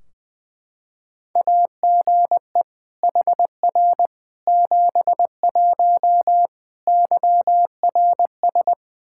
Morse code, heard as a keyed tone, about 20 wpm, 700 Hz. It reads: AGE HR 71 YRS
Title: Practice QSO - Exchange 3 - More Info - 20wpm